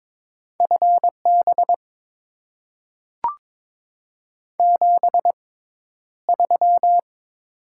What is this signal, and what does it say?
Morse code, heard as a keyed tone, about 22 wpm, spaced as for 11 wpm, 700 Hz. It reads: FB 73
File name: Top 20 QSO Elements - 6x Character Spacing - 22wpm